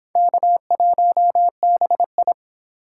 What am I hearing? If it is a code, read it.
Morse code, heard as a keyed tone, about 25 wpm, 700 Hz. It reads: K1BI